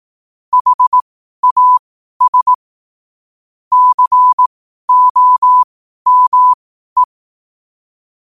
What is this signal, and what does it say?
Morse code, heard as a keyed tone, about 18 wpm, spaced as for 12 wpm, 1000 Hz. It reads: HAS COME
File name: qotdfiles / 12wpm